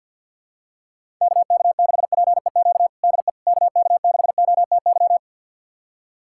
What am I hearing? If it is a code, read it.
Morse code, heard as a keyed tone, about 51 wpm, 700 Hz. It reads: KK6PEQ DE KK6OTY
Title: USA Callsign DE Callsign - 6 Characters - 50wpm